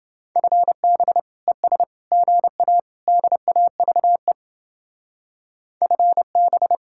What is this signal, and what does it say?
Morse code, heard as a keyed tone, about 30 wpm, 700 Hz. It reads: FB ES GA DAVE FB